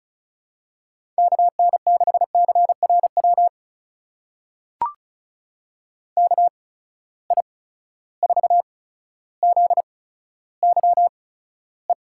Morse code, heard as a keyed tone, about 36 wpm, spaced as for 15 wpm, 700 Hz. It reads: KN6CRW KI4ZYE
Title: USA Call Signs - 6 Characters - 8x Character Spacing - 35wpm